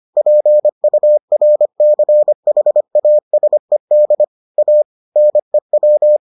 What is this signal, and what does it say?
Morse code, heard as a keyed tone, about 25 wpm, 600 Hz. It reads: PURCHASED A NEW